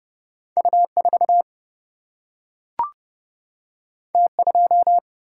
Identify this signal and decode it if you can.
Morse code, heard as a keyed tone, about 30 wpm, 700 Hz. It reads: U4 T2
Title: Single-Letter Single-Number 30wpm